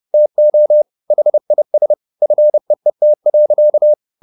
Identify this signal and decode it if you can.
Morse code, heard as a keyed tone, about 30 wpm, 600 Hz. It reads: TO HIS FEET.